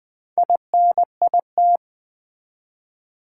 Morse code, heard as a keyed tone, about 20 wpm, 700 Hz. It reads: INIT
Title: Build A Word - 2 Letters At A Time - 8-Letter Words - 20wpm